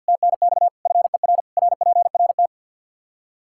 Morse code, heard as a keyed tone, about 52 wpm, 700 Hz. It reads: TNX FER RPRT
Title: Practice QSO - Exchange 1 - Beginning - 50wpm